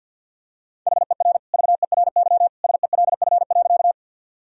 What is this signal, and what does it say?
Morse code, heard as a keyed tone, about 53 wpm, 700 Hz. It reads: HER VERY SELF.
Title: Sentences from Top 1000 Words - Review Part 1 - 50wpm